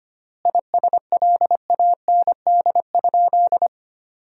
Morse code, heard as a keyed tone, about 25 wpm, 700 Hz. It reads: ISLAND?
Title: 3x Word-Spacing - Sentences from Top 500 Words - 25wpm